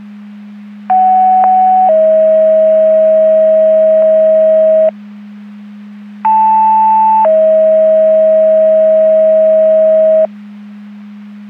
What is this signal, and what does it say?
Two-tone page, heard as a one-second tone, then a three-second tone, about 750 then 640 Hz, about 880 then 640 Hz. It is heard from 0.9 to 10.3 s.
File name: Portage 154.385 Paging